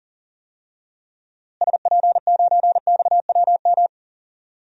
Morse code, heard as a keyed tone, about 40 wpm, 700 Hz. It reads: SP9XWM
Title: DXCC-100 Call Signs - 6 Characters - Repeated 3x - 40wpm